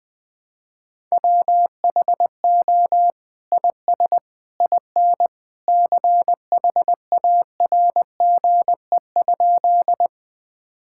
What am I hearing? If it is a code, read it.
Morse code, heard as a keyed tone, about 21 wpm, 700 Hz. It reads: WHO IS IN CHARGE?
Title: Sentences from Top 1000 Words - Review Part 6 - 20wpm